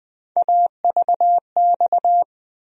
Morse code, heard as a keyed tone, about 20 wpm, 700 Hz. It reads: AVX
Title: USA Call Sign Postfixes - 3 Letters 20wpm